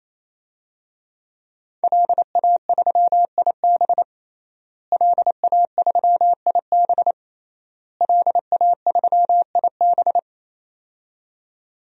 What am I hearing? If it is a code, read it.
Morse code, heard as a keyed tone, about 28 wpm, 700 Hz. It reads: LA3IB LA3IB LA3IB
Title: DXCC-100 Call Signs - 5 Characters - Repeated 3x - 28wpm